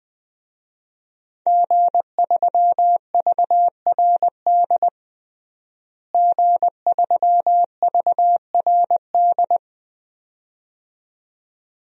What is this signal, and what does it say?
Morse code, heard as a keyed tone, about 20 wpm, 700 Hz. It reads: G3VRD G3VRD
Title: DXCC-100 Call Signs - 5 Characters - Repeated 2x - 20wpm